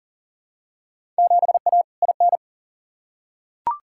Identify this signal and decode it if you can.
Morse code, heard as a keyed tone, about 38 wpm, 700 Hz. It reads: 7A IN
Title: Field Day Exchanges Encore - 40wpm